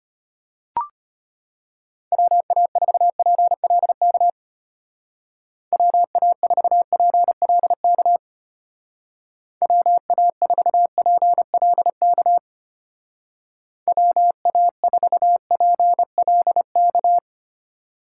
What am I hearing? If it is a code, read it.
Morse code, heard as a keyed tone, about 32 wpm, 700 Hz. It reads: WA4PLK WA4PLK WA4PLK WA4PLK